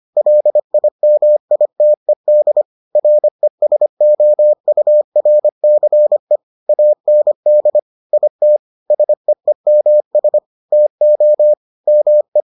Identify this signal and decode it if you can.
Morse code, heard as a keyed tone, about 25 wpm, 600 Hz. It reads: LIMITED RESOURCE AND IT SEEMS TO ME